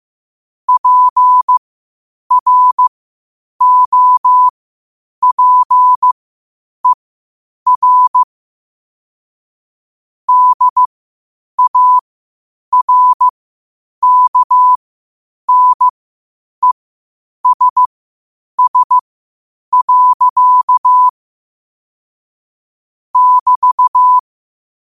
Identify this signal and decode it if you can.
Morse code, heard as a keyed tone, about 15 wpm, spaced as for 8 wpm, 1000 Hz. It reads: PROPER DARKNESS. <BT>
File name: qotdfiles / 8wpm